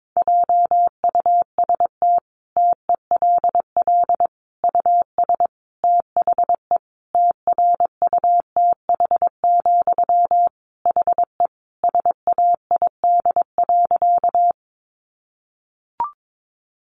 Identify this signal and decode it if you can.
Morse code, heard as a keyed tone, about 22 wpm, 700 Hz. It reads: JUST TELL US THE TRUTH, HE SAID.
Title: Sentences from Top 800 Words 22wpm